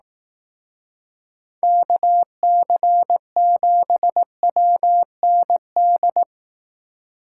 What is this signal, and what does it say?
Morse code, heard as a keyed tone, about 18 wpm, 700 Hz. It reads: KC7WND
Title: Speed Racing - USA Call Signs 15wpm